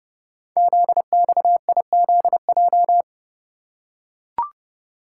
Morse code, heard as a keyed tone, about 30 wpm, 690 Hz. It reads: ZXIZJ
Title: Code Groups - 5 Letters - 30wpm